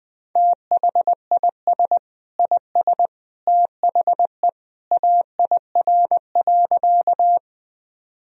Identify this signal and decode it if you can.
Morse code, heard as a keyed tone, about 20 wpm, 700 Hz. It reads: THIS IS THE AIR.
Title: Sentences from Top 1000 Words - Review Part 1 - 20wpm